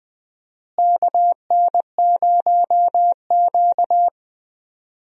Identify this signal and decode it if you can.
Morse code, heard as a keyed tone, about 20 wpm, 700 Hz. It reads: KN0Q
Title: USA Call Signs - 4 Characters - Repeated 3x - 20wpm